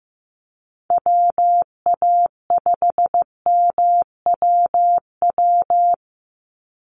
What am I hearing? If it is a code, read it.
Morse code, heard as a keyed tone, about 15 wpm, 700 Hz. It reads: WA5MWW
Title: USA Call Signs 15wpm